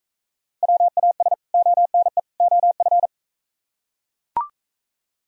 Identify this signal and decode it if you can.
Morse code, heard as a keyed tone, about 41 wpm, 700 Hz. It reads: WAS ONE OF